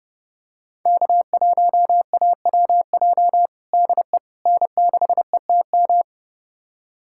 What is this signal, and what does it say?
Morse code, heard as a keyed tone, about 30 wpm, 700 Hz. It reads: K1AWJ DE N6ETM
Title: USA Callsign DE Callsign 30wpm